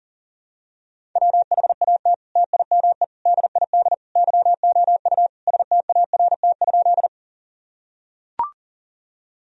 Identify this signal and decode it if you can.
Morse code, heard as a keyed tone, about 37 wpm, 700 Hz. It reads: WHAT TIME DID YOU START?
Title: Sentences from Top 1000 Words - Review Part 1 - 40wpm